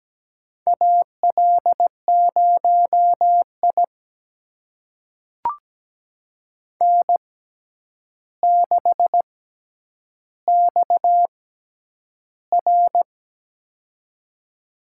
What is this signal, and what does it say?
Morse code, heard as a keyed tone, about 17 wpm, spaced as for 9 wpm, 700 Hz. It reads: AL0I N6XR
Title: USA Call Signs - 4 Characters - 6x Character Spacing - 17wpm